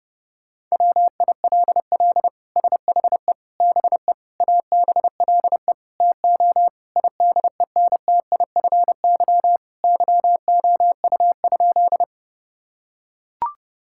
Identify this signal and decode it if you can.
Morse code, heard as a keyed tone, about 30 wpm, 700 Hz. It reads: WILL SHE BE ABLE TO IDENTIFY YOU?
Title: Sentences from Top 1000 Words - Review Part 6 - 30wpm